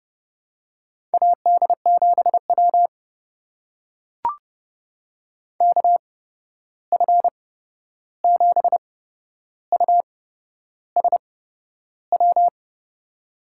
Morse code, heard as a keyed tone, about 30 wpm, spaced as for 11 wpm, 700 Hz. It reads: AD7W KF7USW
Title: USA Call Signs - 8x Character Spacing - 30wpm